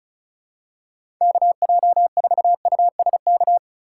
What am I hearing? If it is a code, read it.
Morse code, heard as a keyed tone, about 36 wpm, 700 Hz. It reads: KJ4USK
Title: USA Call Signs - 6 Characters - 3x Character Spacing - 35wpm